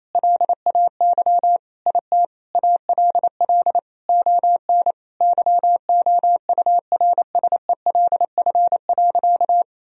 Morse code, heard as a keyed tone, about 27 wpm, 700 Hz. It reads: LAY IT ALL ON YOURSELF.
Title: Sentences from Top 1000 Words - Review Part 2 - 28wpm